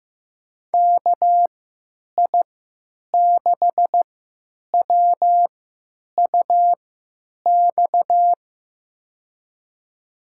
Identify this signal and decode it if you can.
Morse code, heard as a keyed tone, about 15 wpm, spaced as for 9 wpm, 700 Hz. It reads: KI6WUX
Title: USA Call Signs - 3x Character Spacing - 15wpm